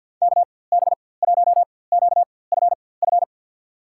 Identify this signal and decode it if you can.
Morse code, heard as a keyed tone, about 52 wpm, spaced as for 24 wpm, 700 Hz. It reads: KB1QFF
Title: USA Call Signs - 6 Characters - 4x Character Spacing - 50wpm